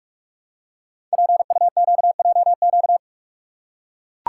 Morse code, heard as a keyed tone, about 44 wpm, 700 Hz. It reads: PUQJQ
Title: Code Groups - 5 Letters - 45wpm